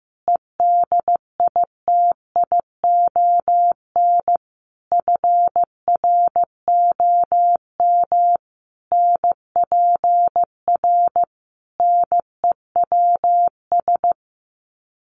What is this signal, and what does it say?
Morse code, heard as a keyed tone, about 15 wpm, 700 Hz. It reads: EDITION FROM NPR NEWS
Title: Sets of 5 Words 15wpm